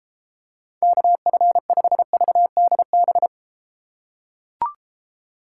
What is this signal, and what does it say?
Morse code, heard as a keyed tone, about 32 wpm, 700 Hz. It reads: KF5VDB